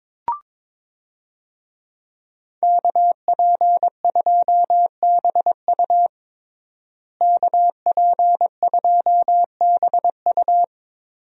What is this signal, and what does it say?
Morse code, heard as a keyed tone, about 22 wpm, 700 Hz. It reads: KP2BU KP2BU
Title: USA Call Signs - 5 Characters - Repeated 3x - 22wpm